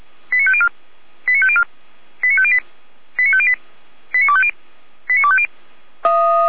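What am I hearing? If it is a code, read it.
Five-tone selective call (ZVEI), heard as quick runs of five tones, tones of about 70 ms. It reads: 89504 89504 89508 89508 89250 89250
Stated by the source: ZVEI five-tone call 89310 received by SDR.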